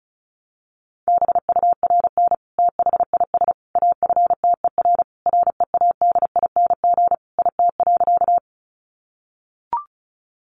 Morse code, heard as a keyed tone, about 34 wpm, 700 Hz. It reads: BURN THIS AFTER READING IT.
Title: Sentences from Top 1300 Words 35wpm